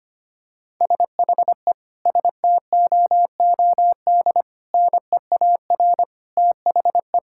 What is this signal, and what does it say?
Morse code, heard as a keyed tone, about 25 wpm, 700 Hz. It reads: SHE STOOD NEAR THE
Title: Sentences from Top 1600 Words 25wpm